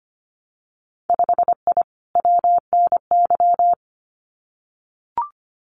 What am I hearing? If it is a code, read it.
Morse code, heard as a keyed tone, about 25 wpm, 700 Hz. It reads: 5I WNY
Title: Winter Field Day Exchanges Encore 25wpm